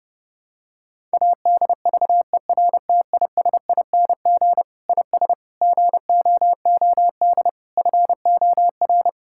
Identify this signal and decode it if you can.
Morse code, heard as a keyed tone, about 30 wpm, 700 Hz. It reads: ADVERTISING IS GOOD FOR
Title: Sentences from Top 1000 Words - Review Part 6 - 30wpm